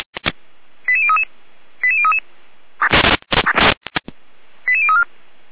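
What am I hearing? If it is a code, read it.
Five-tone selective call (ZVEI), heard as quick runs of five tones, tones of about 70 ms. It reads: 88033 88033 88034
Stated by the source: ZVEI five-tone selective call 88569 received on SDR.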